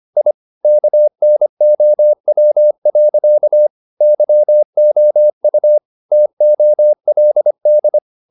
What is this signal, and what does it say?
Morse code, heard as a keyed tone, about 25 wpm, 600 Hz. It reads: I KNOW. YOU TOLD
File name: the old man and the sea part 1 25wpm